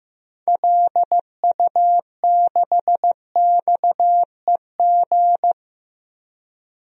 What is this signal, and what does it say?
Morse code, heard as a keyed tone, about 15 wpm, 700 Hz. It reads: LU6XEG
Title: DXCC-100 Call Signs - 6 Characters - 15wpm